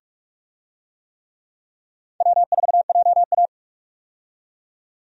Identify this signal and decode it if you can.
Morse code, heard as a keyed tone, about 43 wpm, 700 Hz. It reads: W4JA
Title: USA Call Signs - 4 Characters - 45wpm